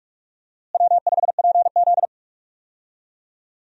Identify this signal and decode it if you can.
Morse code, heard as a keyed tone, about 46 wpm, 700 Hz. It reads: W5PZ
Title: USA Call Signs - 4 Characters - Repeated 3x - 45wpm